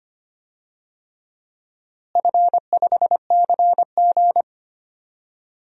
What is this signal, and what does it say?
Morse code, heard as a keyed tone, about 25 wpm, 700 Hz. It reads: F5CG
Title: DXCC-100 Call Signs - 4 Characters - Repeated 3x - 25wpm